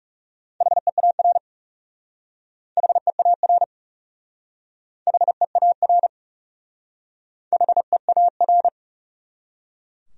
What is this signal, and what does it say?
Morse code, heard as a keyed tone, about 40 wpm, 700 Hz. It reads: HEAR HEAR HEAR HEAR